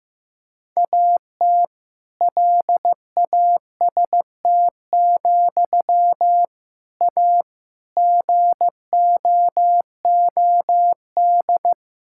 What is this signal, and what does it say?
Morse code, heard as a keyed tone, about 16 wpm, 700 Hz. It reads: AT LAST, A GOOD
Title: Sentences from Top 1600 Words 15wpm